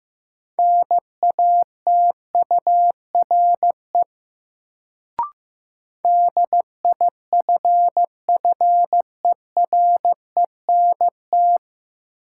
Morse code, heard as a keyed tone, about 16 wpm, 700 Hz. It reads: NATURE DIFFERENT
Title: Sentences from Top 600 Words - New Word Drill - 15wpm